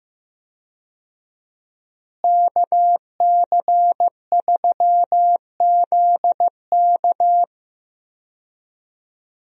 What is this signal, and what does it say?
Morse code, heard as a keyed tone, about 15 wpm, 700 Hz. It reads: KC3ZK
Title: USA Call Signs - 5 Characters - Repeated 2x - 15wpm